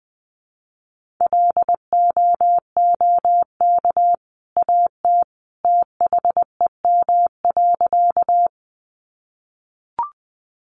Morse code, heard as a keyed tone, about 20 wpm, 700 Hz. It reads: LOOK AT THEM.
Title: Sentences from Top 100 Words 20wpm